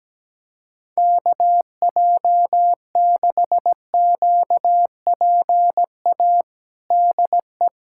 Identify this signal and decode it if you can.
Morse code, heard as a keyed tone, about 17 wpm, 700 Hz. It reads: KJ6QPA DE
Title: USA Callsign DE Callsign - 6 Characters - 17wpm